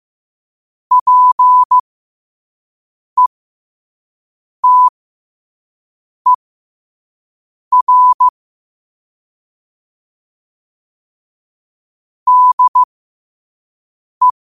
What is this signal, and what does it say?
Morse code, heard as a keyed tone, about 15 wpm, spaced as for 5 wpm, 1000 Hz. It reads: PETER DE
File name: qotdfiles / 5wpm